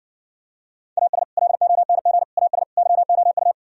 Morse code, heard as a keyed tone, about 69 wpm, 700 Hz. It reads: AS LONG AS YOU